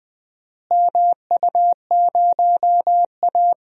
Morse code, heard as a keyed tone, about 20 wpm, 700 Hz. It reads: MU0A
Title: DXCC-100 Call Signs - 4 Characters - Repeated 3x - 20wpm